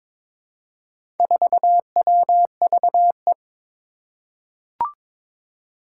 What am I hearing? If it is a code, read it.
Morse code, heard as a keyed tone, about 22 wpm, 700 Hz. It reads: 4WVE
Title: Four Challenging Letter-Number Combinations 22wpm